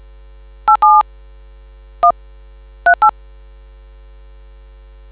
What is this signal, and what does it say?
DTMF, heard as telephone touch tones, tones of about 100 ms. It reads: 87 138